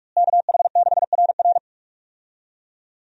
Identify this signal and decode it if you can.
Morse code, heard as a keyed tone, about 45 wpm, 700 Hz. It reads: KH6RR
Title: USA Call Signs - 5 Characters - Repeated 2x - 45wpm